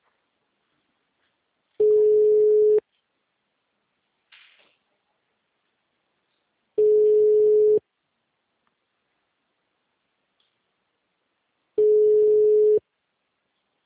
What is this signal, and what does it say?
Ringback tone (Europe, 425 Hz).